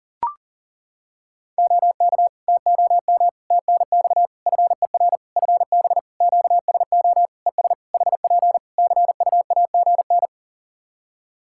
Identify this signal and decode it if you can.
Morse code, heard as a keyed tone, about 40 wpm, 700 Hz. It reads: OK TOM TNX FER FB QSO ES HP CUAGN
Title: Practice QSO - Exchange 4 - Ending - 40wpm